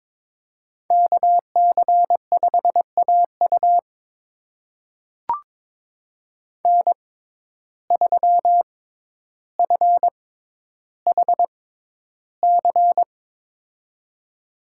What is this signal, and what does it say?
Morse code, heard as a keyed tone, about 22 wpm, spaced as for 11 wpm, 700 Hz. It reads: KC5AU N3FHC
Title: USA Call Signs - 5 Characters - 6x Character Spacing - 22wpm